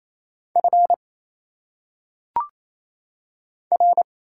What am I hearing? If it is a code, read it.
Morse code, heard as a keyed tone, about 27 wpm, 700 Hz. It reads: F R
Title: Single Letters - Phonetic Pronunciation 28wpm